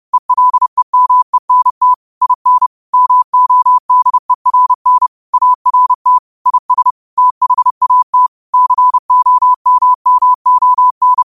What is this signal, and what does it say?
Morse code, heard as a keyed tone, about 30 wpm, 1000 Hz. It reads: ELEMENT IN MODERN ART IS THAT COMMON